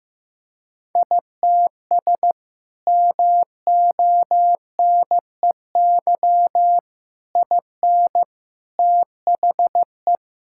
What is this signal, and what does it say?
Morse code, heard as a keyed tone, about 15 wpm, 700 Hz. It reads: ITS MONEY IN THE
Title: Sentences from Top 1000 Words - Review Part 6 - 15wpm